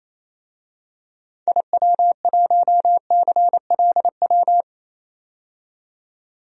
Morse code, heard as a keyed tone, about 28 wpm, 700 Hz. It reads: IW1CLW
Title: DXCC-100 Call Signs - 6 Characters - 28wpm